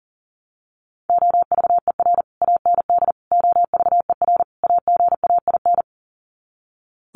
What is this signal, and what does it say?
Morse code, heard as a keyed tone, about 40 wpm, 700 Hz. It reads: OVER AND OVER AGAIN
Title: Sets of 4 Words 40wpm